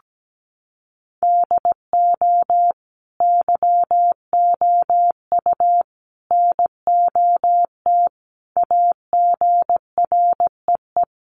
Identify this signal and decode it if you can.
Morse code, heard as a keyed tone, about 17 wpm, 700 Hz. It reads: DO YOU NOT AGREE
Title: Sentences from Top 1100 Words 17wpm